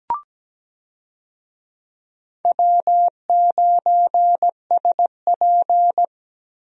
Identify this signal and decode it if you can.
Morse code, heard as a keyed tone, about 17 wpm, 700 Hz. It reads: W9SP
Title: USA Call Signs - 4 Characters - Repeated 3x - 17wpm